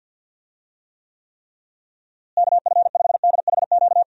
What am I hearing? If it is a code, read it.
Morse code, heard as a keyed tone, about 51 wpm, 700 Hz. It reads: KF5DHQ
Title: USA Call Signs - 6 Characters - Repeated 3x - 50wpm